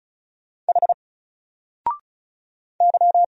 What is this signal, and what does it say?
Morse code, heard as a keyed tone, about 36 wpm, 700 Hz. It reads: H Y
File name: Single-Letter-Number-35wpm